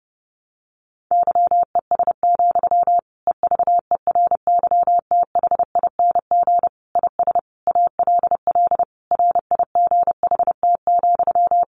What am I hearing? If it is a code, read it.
Morse code, heard as a keyed tone, about 28 wpm, 700 Hz. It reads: YES, EVERYTHING IS ALL RIGHT,
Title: Sentences from Top 1100 Words 30wpm